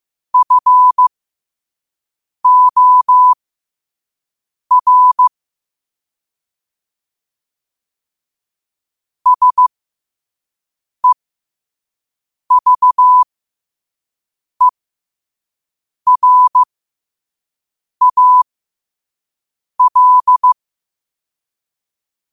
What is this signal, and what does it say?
Morse code, heard as a keyed tone, about 15 wpm, spaced as for 5 wpm, 1000 Hz. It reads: FOR SEVERAL
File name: qotdfiles / 5wpm